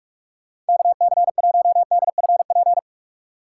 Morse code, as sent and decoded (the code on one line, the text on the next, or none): -.- -.-. .---- -.. ..-. .--.
KC1DFP